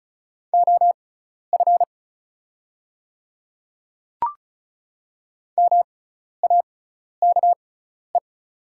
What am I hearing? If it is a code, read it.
Morse code, heard as a keyed tone, about 36 wpm, spaced as for 12 wpm, 700 Hz. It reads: OF MAKE
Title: 6x Character Spacing - Top 100 Words - 35wpm